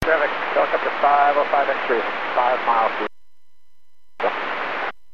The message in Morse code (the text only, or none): none